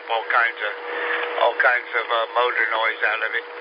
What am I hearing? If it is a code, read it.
Emergency Alert System (SAME) tones, heard as harsh buzzing bursts, with speech. It is heard from 0.3 to 3.4 s.